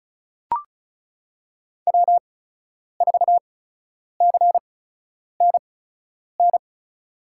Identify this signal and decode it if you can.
Morse code, heard as a keyed tone, about 36 wpm, spaced as for 10 wpm, 700 Hz. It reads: W4CNN
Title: USA Call Signs - 5 Characters - 8x Character Spacing - 35wpm